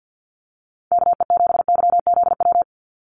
Morse code, heard as a keyed tone, about 48 wpm, 700 Hz. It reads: KE7YZW
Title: USA Call Signs 50wpm